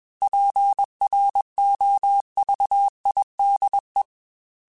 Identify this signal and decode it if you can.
Morse code, heard as a keyed tone, about 21 wpm, 800 Hz. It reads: PROVIDE
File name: Technical Subjects / Technical Subjects-mp3s / Sw Pwr Sup @20